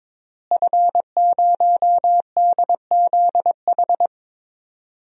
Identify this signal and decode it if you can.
Morse code, heard as a keyed tone, about 22 wpm, 700 Hz. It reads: F0DZH
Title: DXCC-100 Call Signs - 5 Characters - Repeated 2x - 22wpm